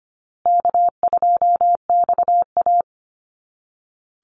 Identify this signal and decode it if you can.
Morse code, heard as a keyed tone, about 25 wpm, 700 Hz. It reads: K2XA
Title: Easy Call Signs Repeated 3x - 25wpm